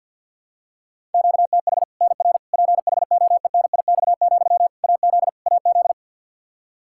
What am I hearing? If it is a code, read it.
Morse code, heard as a keyed tone, about 52 wpm, 700 Hz. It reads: QTH NR PHOENIX, AZ AZ
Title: K1USN SST QSO Practice 50wpm